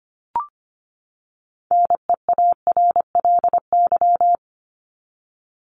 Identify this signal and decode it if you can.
Morse code, heard as a keyed tone, about 25 wpm, 700 Hz. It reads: NEARLY
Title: LY Suffix - Top 100 Words with Base Words 25wpm